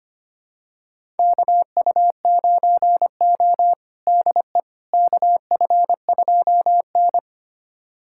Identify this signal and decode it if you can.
Morse code, heard as a keyed tone, about 25 wpm, 700 Hz. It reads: KU9O DE KF2N
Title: USA Callsign DE Callsign - 4 Characters - 25wpm